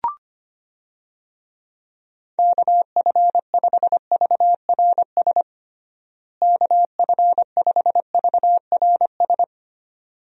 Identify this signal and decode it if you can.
Morse code, heard as a keyed tone, about 25 wpm, 700 Hz. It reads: KF5VRS KF5VRS
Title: USA Call Signs - 6 Characters - Repeated 2x - 25wpm